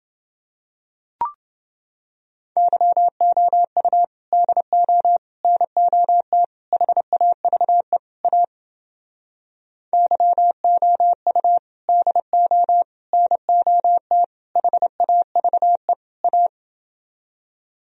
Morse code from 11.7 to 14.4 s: DO NOT

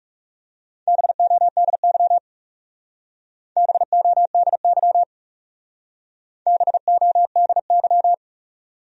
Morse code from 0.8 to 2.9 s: BODY

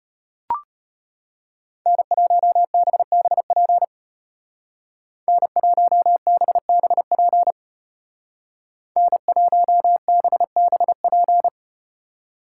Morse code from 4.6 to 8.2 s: N1BBP

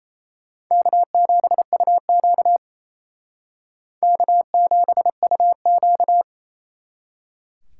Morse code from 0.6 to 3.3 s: K7UQ

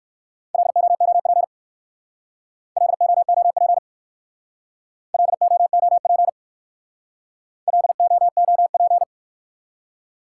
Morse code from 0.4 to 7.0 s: LOOP LOOP LOOP

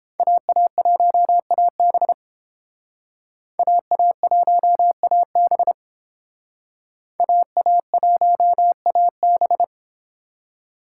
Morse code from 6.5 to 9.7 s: AA1AB

